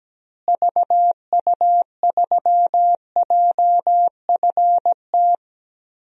Morse code, heard as a keyed tone, about 17 wpm, 700 Hz. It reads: VU3JFT
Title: DXCC-100 Call Signs - 6 Characters - Repeated 3x - 17wpm